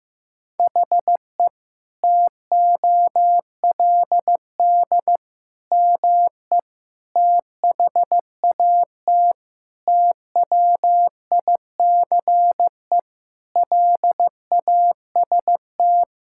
Morse code, heard as a keyed tone, about 16 wpm, 700 Hz. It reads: HE TOLD ME THAT TWICE LAST
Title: Sentences from Top 1600 Words 15wpm